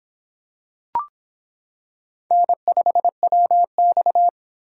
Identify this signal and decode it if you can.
Morse code, heard as a keyed tone, about 26 wpm, 700 Hz. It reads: N5WX